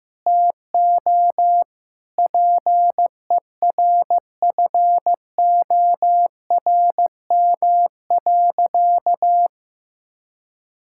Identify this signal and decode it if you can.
Morse code, heard as a keyed tone, about 15 wpm, 700 Hz. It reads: TO PERFORM.
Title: Sentences from Top 1000 Words - Review Part 5 - 15wpm